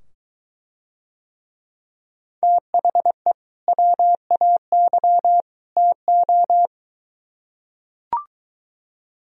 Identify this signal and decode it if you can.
Morse code, heard as a keyed tone, about 23 wpm, 700 Hz. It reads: THE WAY TO